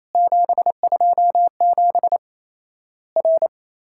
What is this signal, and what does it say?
Morse code, heard as a keyed tone, about 28 wpm, 700 Hz. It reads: 727 R
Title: Worked All Europe DX Contest - QTC Exchange - 28wpm